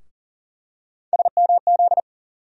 Morse code, heard as a keyed tone, about 40 wpm, 700 Hz. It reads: SMZ
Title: USA Call Sign Postfixes - 3 Letters 40wpm